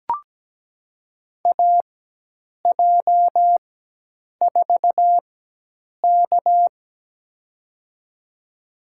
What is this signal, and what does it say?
Morse code, heard as a keyed tone, about 17 wpm, spaced as for 8 wpm, 700 Hz. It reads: AJ4K
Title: USA Call Signs - 4 Characters - 4x Character Spacing - 17wpm